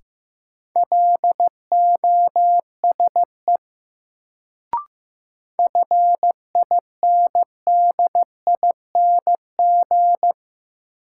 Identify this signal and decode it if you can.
Morse code, heard as a keyed tone, about 15 wpm, 700 Hz. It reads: LOSE FINDING
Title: Sentences from Top 1100 Words - New Word Drill - 15wpm